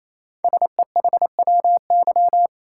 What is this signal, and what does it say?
Morse code, heard as a keyed tone, about 28 wpm, 700 Hz. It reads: SEHWY
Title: Code Groups - 5 Letters - 28wpm